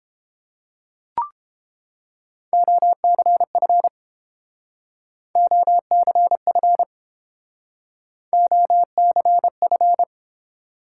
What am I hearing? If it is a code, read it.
Morse code, heard as a keyed tone, about 30 wpm, 700 Hz. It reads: OCF OCF OCF